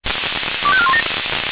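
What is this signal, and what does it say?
Five-tone selective call (ZVEI), heard as quick runs of five tones, tones of about 70 ms. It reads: 25417
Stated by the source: ZVEI five-tone selective call received with an SDR.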